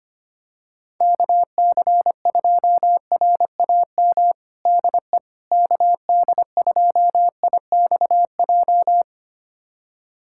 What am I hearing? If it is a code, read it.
Morse code, heard as a keyed tone, about 25 wpm, 700 Hz. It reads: KC2RAM DE KD2IXJ
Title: USA Callsign DE Callsign - 6 Characters - 25wpm